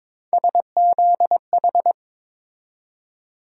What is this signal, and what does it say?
Morse code, heard as a keyed tone, about 22 wpm, 700 Hz. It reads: SZH
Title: USA Call Sign Postfixes - 3 Letters 22wpm